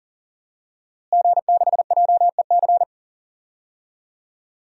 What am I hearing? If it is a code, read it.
Morse code, heard as a keyed tone, about 40 wpm, 700 Hz. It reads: G6JEC
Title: DXCC-100 Call Signs - 5 Characters - Repeated 2x - 40wpm